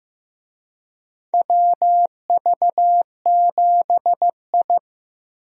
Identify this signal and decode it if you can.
Morse code, heard as a keyed tone, about 15 wpm, 700 Hz. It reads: WV7I
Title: USA Call Signs - 4 Characters - Repeated 3x - 15wpm